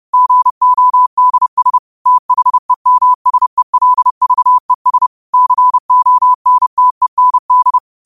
Morse code, heard as a keyed tone, about 30 wpm, 1000 Hz. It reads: GODS THEMSELVES CONTEND
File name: qotdfiles / 30wpm